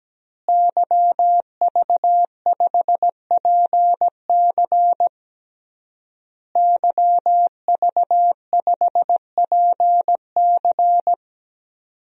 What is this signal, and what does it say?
Morse code, heard as a keyed tone, about 17 wpm, 700 Hz. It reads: YV5PC YV5PC
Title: DXCC-100 Call Signs - 5 Characters - Repeated 3x - 17wpm